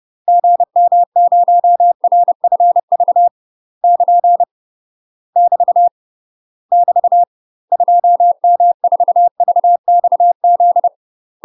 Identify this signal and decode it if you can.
Morse code, heard as a keyed tone, about 30 wpm, 700 Hz. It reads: GM0RFV <KN> <BT> <BT> 2M4VXZ
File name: UK MASTER QSOs TRAINING/30WPM